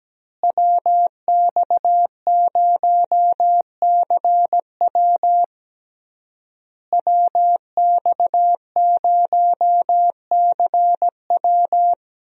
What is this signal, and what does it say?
Morse code, heard as a keyed tone, about 17 wpm, 700 Hz. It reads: WX0CW WX0CW
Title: USA Call Signs - 5 Characters - Repeated 3x - 17wpm